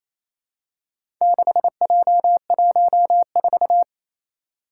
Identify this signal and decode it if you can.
Morse code, heard as a keyed tone, about 28 wpm, 700 Hz. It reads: 6J14
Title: Four Challenging Letter-Number Combinations 28wpm